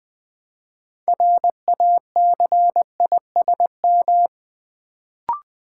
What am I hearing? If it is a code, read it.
Morse code, heard as a keyed tone, about 20 wpm, 700 Hz. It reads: RACISM
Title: Build A Word - 3 Letters At A Time - 6-Letter Words - 20wpm